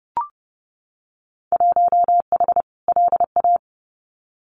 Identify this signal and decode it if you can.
Morse code, heard as a keyed tone, about 30 wpm, 700 Hz. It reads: 1H LA
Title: Winter Field Day Exchanges 30wpm